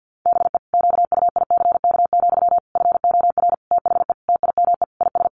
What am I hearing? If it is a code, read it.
Morse code, heard as a keyed tone, about 50 wpm, 700 Hz. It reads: BE QUICK, FOR THE TIME IS
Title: Sentences from Top 500 Words 50wpm